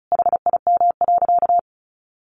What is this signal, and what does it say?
Morse code, heard as a keyed tone, about 34 wpm, 700 Hz. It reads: HIM.
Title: Sentences from Top 700 Words 35wpm